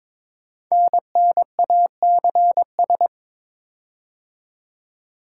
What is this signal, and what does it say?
Morse code, heard as a keyed tone, about 22 wpm, 700 Hz. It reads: NNACS
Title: Code Groups - 5 Letters - 22wpm